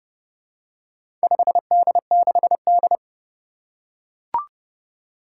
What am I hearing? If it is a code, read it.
Morse code, heard as a keyed tone, about 30 wpm, 700 Hz. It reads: 5D6D
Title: Four Challenging Letter-Number Combinations 30wpm